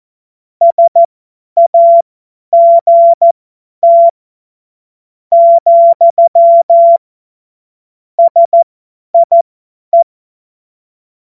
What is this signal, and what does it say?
Morse code, heard as a keyed tone, about 14 wpm, spaced as for 9 wpm, 680 Hz. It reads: SAGT, SIE
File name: Morsekurs / Witze / Witz 21.November 2023 50 F70